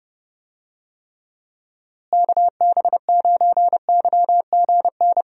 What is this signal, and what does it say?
Morse code, heard as a keyed tone, about 30 wpm, 700 Hz. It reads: KB9YGN
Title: USA Call Signs - 6 Characters - Repeated 3x - 30wpm